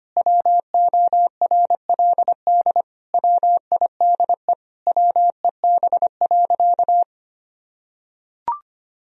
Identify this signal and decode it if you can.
Morse code, heard as a keyed tone, about 25 wpm, 700 Hz. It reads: WORLD WIDE WEB.
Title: Sentences from Top 1000 Words - Review Part 4 - 25wpm